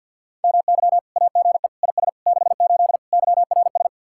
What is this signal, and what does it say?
Morse code, heard as a keyed tone, about 64 wpm, 700 Hz. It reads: MY AGE IS 68 YRS